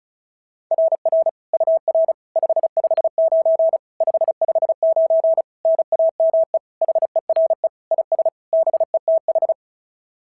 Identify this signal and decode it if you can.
Morse code, heard as a keyed tone, about 36 wpm, 690 Hz. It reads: RR UR 559 559 NAME HERE IS BETH
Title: SOTA QSO Practice 35wpm